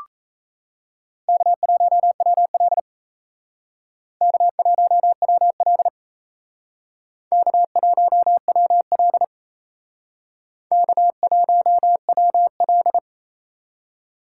Morse code, heard as a keyed tone, about 36 wpm, 700 Hz. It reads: K1WL K1WL K1WL K1WL